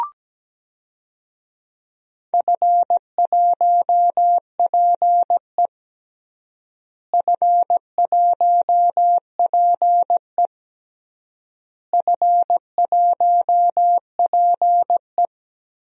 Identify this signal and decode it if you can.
Morse code, heard as a keyed tone, about 17 wpm, 700 Hz. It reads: F1PE F1PE F1PE
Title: DXCC-100 Call Signs - 4 Characters - Repeated 3x - 17wpm